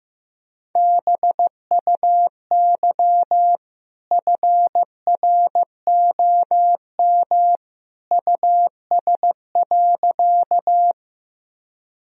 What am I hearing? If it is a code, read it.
Morse code, heard as a keyed tone, about 15 wpm, 700 Hz. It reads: BUY FROM US.
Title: Sentences from Top 1000 Words - Review Part 4 - 15wpm